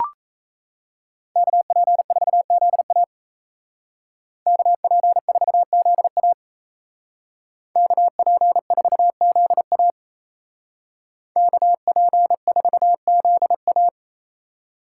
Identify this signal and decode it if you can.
Morse code, heard as a keyed tone, about 35 wpm, 700 Hz. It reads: KP4ZA KP4ZA KP4ZA KP4ZA
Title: Speed Racing - USA 2x2 Call Signs 28wpm